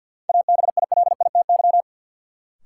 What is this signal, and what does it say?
Morse code, heard as a keyed tone, about 53 wpm, 700 Hz. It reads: ABILITY
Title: Sets of 3 Words - New Word Drill - 50wpm